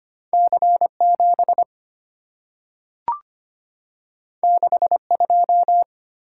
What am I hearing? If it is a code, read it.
Morse code, heard as a keyed tone, about 25 wpm, 700 Hz. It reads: C7 62
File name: Two Letter-Number Combinations - 25wpm